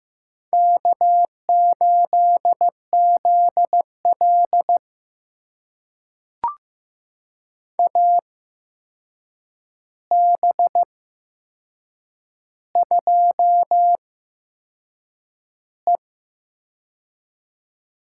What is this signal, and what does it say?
Morse code, heard as a keyed tone, about 15 wpm, spaced as for 6 wpm, 700 Hz. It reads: K8ZL AB2E
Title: USA Call Signs - 4 Characters - 8x Character Spacing - 15wpm